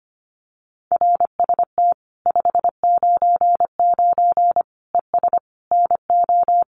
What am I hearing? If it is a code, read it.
Morse code, heard as a keyed tone, about 25 wpm, 700 Hz. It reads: RST 599 ES NO
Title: Common QSO Phrases 25wpm